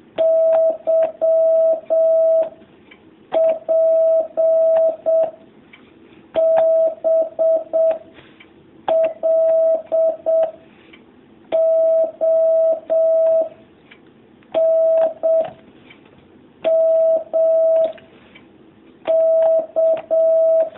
Morse code, heard as a keyed tone, about 7 wpm, 650 Hz. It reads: Y P B L O N M K
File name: jh8cbh / 2019cw